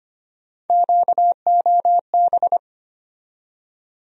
Morse code, heard as a keyed tone, about 25 wpm, 700 Hz. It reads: QOB
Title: USA Call Sign Postfixes - 3 Letters 25wpm